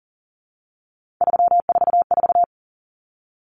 Morse code, heard as a keyed tone, about 40 wpm, 700 Hz. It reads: 344
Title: Numbers - 3-Digit 40wpm